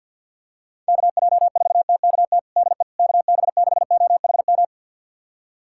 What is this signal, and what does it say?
Morse code, heard as a keyed tone, about 52 wpm, 700 Hz. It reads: KJ4TKT DE KB6OHM
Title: USA Callsign DE Callsign 50wpm